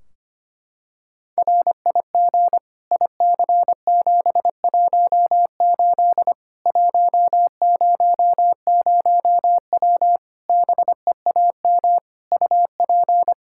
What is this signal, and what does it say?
Morse code, heard as a keyed tone, about 25 wpm, 700 Hz. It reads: RIG IC718 100W BEAM UP
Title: K1USN SST QSO Practice 25wpm